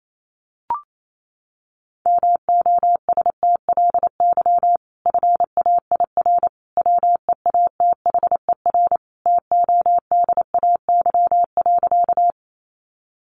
Morse code, heard as a keyed tone, about 28 wpm, 700 Hz. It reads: MOSTLY FAIR WEATHER TODAY.
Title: Sentences from Top 1300 Words 28wpm